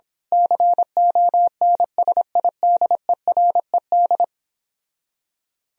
Morse code, heard as a keyed tone, about 25 wpm, 700 Hz. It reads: CONSIDERED
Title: Speed-Racing - Sets of 5 Words 22wpm